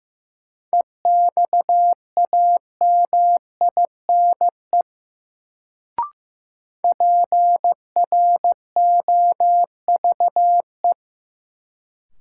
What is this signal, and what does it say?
Morse code, heard as a keyed tone, about 16 wpm, 700 Hz. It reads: EXAMINE PROVE
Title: Sentences from Top 1600 Words - New Word Drill Alt - 15wpm